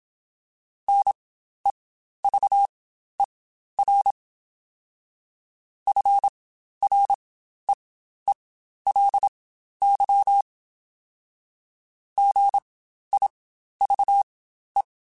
Morse code, heard as a keyed tone, about 27 wpm, spaced as for 12 wpm, 800 Hz. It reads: NEVER FREELY GIVE
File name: Quotes Spaced / Quotes Spaced-mp3s / Advice sp @25